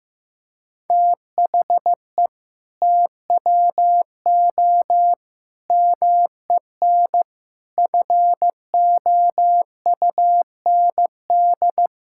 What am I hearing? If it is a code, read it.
Morse code, heard as a keyed tone, about 15 wpm, 700 Hz. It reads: THE TWO MEN FOUND
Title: Sentences from Top 1000 Words - Review Part 5 - 15wpm